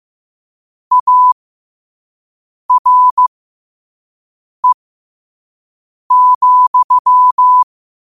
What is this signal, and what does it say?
Morse code, heard as a keyed tone, about 15 wpm, spaced as for 5 wpm, 1000 Hz. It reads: ARE,
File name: qotdfiles / 5wpm